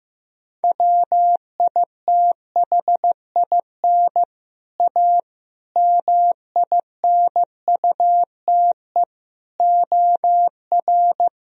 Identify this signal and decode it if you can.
Morse code, heard as a keyed tone, about 15 wpm, 700 Hz. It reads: WITHIN A MINUTE OR
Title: Sentences from Top 1000 Words - Review Part 4 - 15wpm